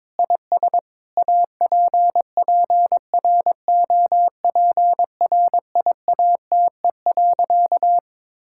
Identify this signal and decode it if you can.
Morse code, heard as a keyed tone, about 22 wpm, 700 Hz. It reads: IS APPROPRIATE.
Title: Sentences from Top 1000 Words - Review Part 5 - 22wpm